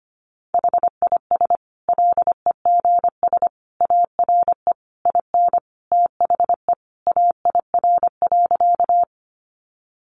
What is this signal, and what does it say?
Morse code, heard as a keyed tone, about 25 wpm, 700 Hz. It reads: HIS LEGS ARE IN THE AIR.
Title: Additional Easy Phrases 25wpm